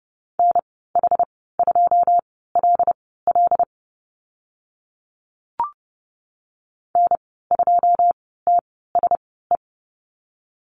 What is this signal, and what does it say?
Morse code, heard as a keyed tone, about 30 wpm, spaced as for 17 wpm, 700 Hz. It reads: NH2LL N2TSE
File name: Lesson 14g - Number 2 - Call Signs 30wpm